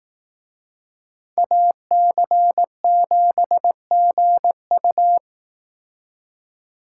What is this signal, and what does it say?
Morse code, heard as a keyed tone, about 18 wpm, 700 Hz. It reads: AC7GU
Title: Speed Racing - USA Call Signs 15wpm